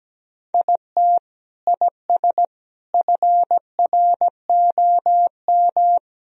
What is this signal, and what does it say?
Morse code, heard as a keyed tone, about 17 wpm, 700 Hz. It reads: IT IS FROM
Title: Sentences from Top 1000 Words - Review Part 1 - 17wpm